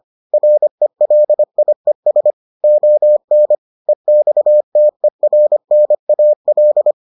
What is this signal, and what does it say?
Morse code, heard as a keyed tone, about 25 wpm, 600 Hz. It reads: RELIES ON EXTERNAL